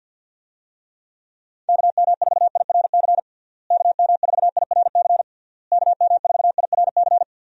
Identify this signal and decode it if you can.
Morse code, heard as a keyed tone, about 52 wpm, 700 Hz. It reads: KM4IRC KM4IRC KM4IRC
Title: USA Call Signs - 6 Characters - Repeated 3x - 50wpm